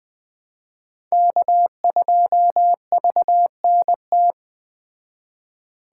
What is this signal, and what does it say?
Morse code, heard as a keyed tone, about 20 wpm, 700 Hz. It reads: K2VNT